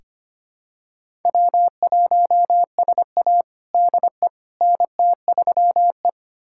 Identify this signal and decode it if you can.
Morse code, heard as a keyed tone, about 25 wpm, 700 Hz. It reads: W1SA DE NT3E
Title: USA Callsign DE Callsign - 4 Characters - 25wpm